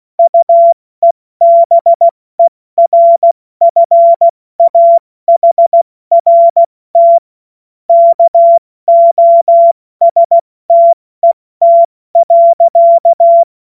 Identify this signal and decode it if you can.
Morse code, heard as a keyed tone, about 16 wpm, 680 Hz. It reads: UEBERFAHRT KOSTET.
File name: Morsekurs / Witze / Witz 21. Februar 2023 70 F80